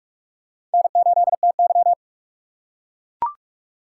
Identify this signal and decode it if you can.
Morse code, heard as a keyed tone, about 44 wpm, 700 Hz. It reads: N8TY